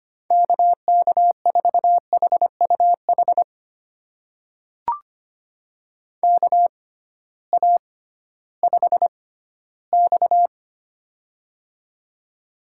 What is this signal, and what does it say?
Morse code, heard as a keyed tone, about 25 wpm, spaced as for 15 wpm, 700 Hz. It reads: KK4HUH KA5X
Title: USA Call Signs - 6x Character Spacing - 25wpm